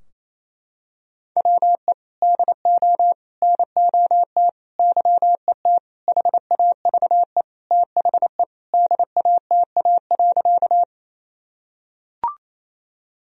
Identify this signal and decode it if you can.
Morse code, heard as a keyed tone, about 28 wpm, 700 Hz. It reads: WE DO NOT YET HAVE THE DATA.
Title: Sentences from Top 1000 Words - Review Part 3 - 28wpm